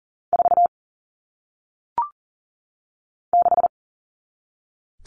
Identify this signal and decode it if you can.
Morse code, heard as a keyed tone, about 39 wpm, 700 Hz. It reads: 4 6
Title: All Cut Numbers - 1-Digit 40wpm